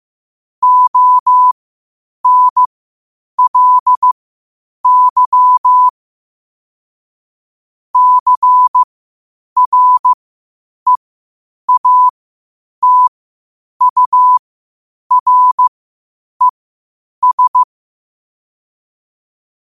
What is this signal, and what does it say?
Morse code, heard as a keyed tone, about 15 wpm, spaced as for 8 wpm, 1000 Hz. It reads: ONLY CREATURES
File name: qotdfiles / 8wpm